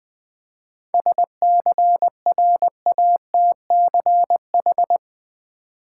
Morse code, heard as a keyed tone, about 20 wpm, 700 Hz. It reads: SCRATCH
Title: Seven Letter Words - 20wpm